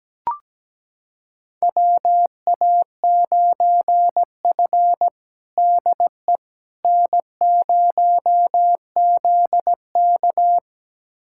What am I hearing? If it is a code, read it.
Morse code, heard as a keyed tone, about 17 wpm, 700 Hz. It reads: WA9F DE N0ZK
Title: USA Callsign DE Callsign - 4 Characters - 17wpm